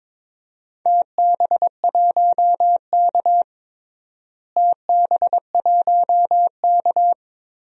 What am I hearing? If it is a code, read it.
Morse code, heard as a keyed tone, about 22 wpm, 700 Hz. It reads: TB1K TB1K
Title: DXCC-100 Call Signs - 4 Characters - Repeated 2x - 22wpm